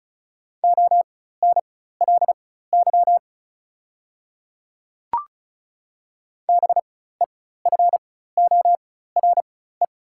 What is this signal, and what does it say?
Morse code, heard as a keyed tone, about 36 wpm, spaced as for 16 wpm, 700 Hz. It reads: ONLY BEFORE
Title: 4x Character Spacing - Top 200 Words - 35wpm